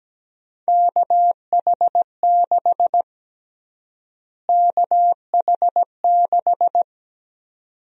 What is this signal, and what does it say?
Morse code, heard as a keyed tone, about 17 wpm, 700 Hz. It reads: KH6 KH6
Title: USA Call Sign Prefixes - Repeated 2x - 17wpm